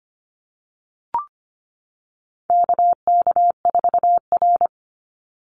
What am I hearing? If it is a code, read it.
Morse code, heard as a keyed tone, about 25 wpm, 700 Hz. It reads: KK4R
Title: Call Signs - 25wpm